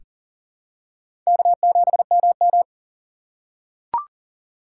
Morse code, heard as a keyed tone, about 40 wpm, 700 Hz. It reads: K7MM
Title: USA Call Signs - 3x Character Spacing - 40wpm